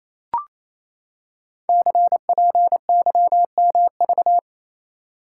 Morse code, heard as a keyed tone, about 29 wpm, 700 Hz. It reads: CPYMV
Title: Code Groups - 5 Letter:Numbers - 28wpm